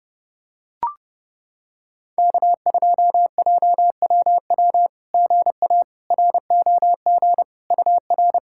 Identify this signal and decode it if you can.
Morse code, heard as a keyed tone, about 30 wpm, 690 Hz. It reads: K2JWW GA ROG UR
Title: SOTA QSO Practice 30wpm